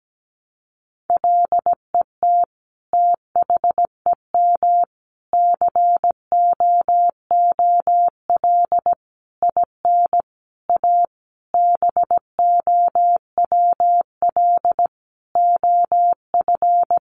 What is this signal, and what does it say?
Morse code, heard as a keyed tone, about 17 wpm, 700 Hz. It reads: LET THEM COOL IN A BOWL OF
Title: Sentences from Top 1200 Words 17wpm